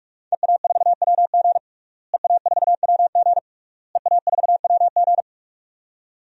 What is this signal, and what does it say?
Morse code, heard as a keyed tone, about 46 wpm, 700 Hz. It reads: EA4WG EA4WG EA4WG
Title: DXCC-100 Call Signs - 5 Characters - Repeated 3x - 45wpm